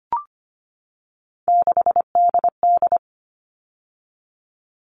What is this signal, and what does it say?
Morse code, heard as a keyed tone, about 25 wpm, 700 Hz. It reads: 6DD
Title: All Cut Numbers - 3-Digit 25wpm